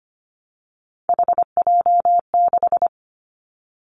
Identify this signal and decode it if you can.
Morse code, heard as a keyed tone, about 25 wpm, 700 Hz. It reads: HJ6
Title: Callsign Prefixes for Top 100 DXCC - 25wpm